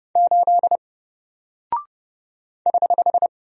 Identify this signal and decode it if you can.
Morse code, heard as a keyed tone, about 30 wpm, 700 Hz. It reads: 8 <HH>
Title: Single Character (Letter:Number:Punctuation:Prosign) 30wpm